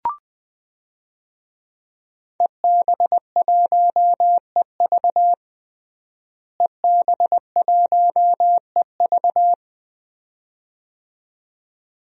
Morse code, heard as a keyed tone, about 21 wpm, 700 Hz. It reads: EB1EV EB1EV
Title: DXCC-100 Call Signs - 5 Characters - Repeated 2x - 20wpm